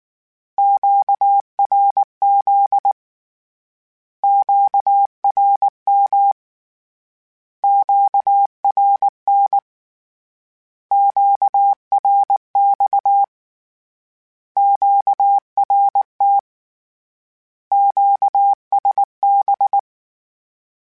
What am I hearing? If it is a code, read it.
Morse code, heard as a keyed tone, about 19 wpm, 800 Hz. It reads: QRZ QRM QRN QRX QRT QSB
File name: Short Runs / Short Runs-mp3s / Words @18